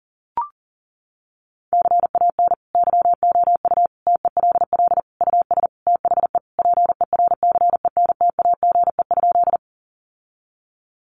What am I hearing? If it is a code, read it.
Morse code, heard as a keyed tone, about 40 wpm, 700 Hz. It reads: CAN YOU TELL US THE PERCENTAGE?
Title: Sentences from Top 1300 Words 40wpm